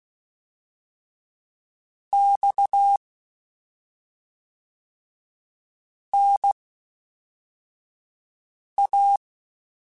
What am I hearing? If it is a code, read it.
Morse code, heard as a keyed tone, about 16 wpm, spaced as for 3 wpm, 800 Hz. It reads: XNA
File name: CW Comparisons / CW Comparisons-mp3s / CW Comparisons Broken Characters @5